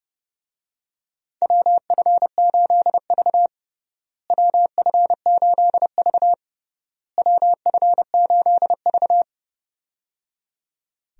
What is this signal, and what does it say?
Morse code, heard as a keyed tone, about 30 wpm, 700 Hz. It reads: WF8V WF8V WF8V
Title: USA Call Signs - 4 Characters - Repeated 3x - 30wpm